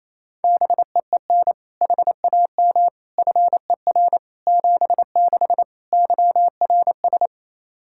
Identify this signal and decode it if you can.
Morse code, heard as a keyed tone, about 27 wpm, 700 Hz. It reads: BEEN HAM FER 76 YRS
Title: K1USN SST QSO Practice 28wpm